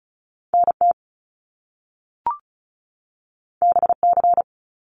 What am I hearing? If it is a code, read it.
Morse code, heard as a keyed tone, about 36 wpm, 700 Hz. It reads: NT BC
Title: Canadian Provinces and Territories 35wpm